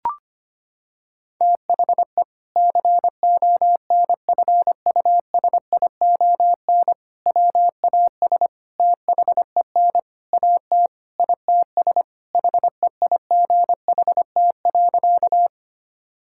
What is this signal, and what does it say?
Morse code, heard as a keyed tone, about 25 wpm, 700 Hz. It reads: THE CONFUSION WAS THEN AT ITS HEIGHT.
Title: Sentences from Top 1600 Words 25wpm